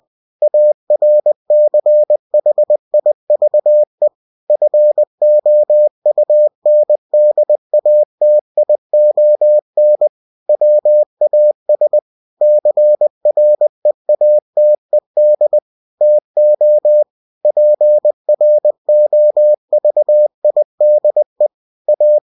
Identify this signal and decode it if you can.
Morse code, heard as a keyed tone, about 20 wpm, 600 Hz. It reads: ARCHIVE FOUNDATION WAS CREATED TO PROVIDE A
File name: cw-books / winnie-the-pooh / winnie-20wpm-page-0120